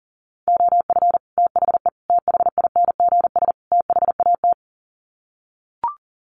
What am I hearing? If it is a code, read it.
Morse code, heard as a keyed tone, about 40 wpm, 700 Hz. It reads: OF THE THINGS THAT
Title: Sets of 4 Words 40wpm